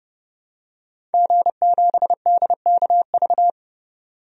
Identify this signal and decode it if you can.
Morse code, heard as a keyed tone, about 30 wpm, 700 Hz. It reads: G7DKV
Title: DXCC-100 Call Signs - 5 Characters - 30wpm